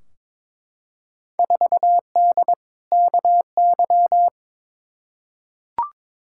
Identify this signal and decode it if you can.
Morse code, heard as a keyed tone, about 22 wpm, 700 Hz. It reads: 4D KY
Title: Field Day Exchanges Encore - 22wpm